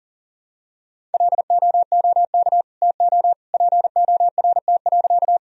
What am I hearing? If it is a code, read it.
Morse code, heard as a keyed tone, about 40 wpm, 700 Hz. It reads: LOOK TO PORT.
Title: Sentences from Top 1000 Words - Review Part 1 - 40wpm